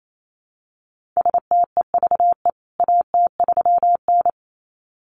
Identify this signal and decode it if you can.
Morse code, heard as a keyed tone, about 28 wpm, 700 Hz. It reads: STEVE AT3N
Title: CWT Exchange 28wpm